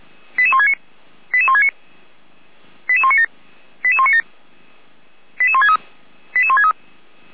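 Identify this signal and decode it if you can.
Five-tone selective call (ZVEI), heard as quick runs of five tones, tones of about 70 ms. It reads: 88169 88169 88197 88197 88163 88163
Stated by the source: ZVEI five-tone call received by SDR.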